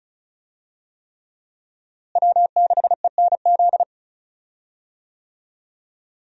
Morse code, heard as a keyed tone, about 36 wpm, 700 Hz. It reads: W6ENZ
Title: USA Call Signs - 5 Characters - Repeated 3x - 35wpm